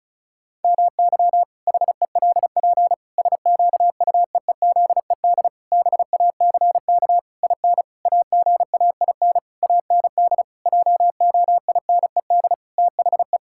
Morse code, heard as a keyed tone, about 36 wpm, 700 Hz. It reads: MY HELP SQUEEZED BACK IN AGAIN AND JOINED THE
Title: Pangram Challenge v2 - 35wpm